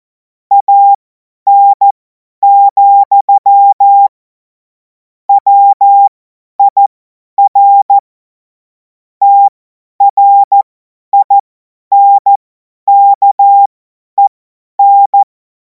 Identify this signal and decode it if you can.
Morse code, heard as a keyed tone, about 14 wpm, spaced as for 10 wpm, 800 Hz. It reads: AN, WIR TRINKEN
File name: Morsekurs / Witze / Witz 14. April 2026 50 F70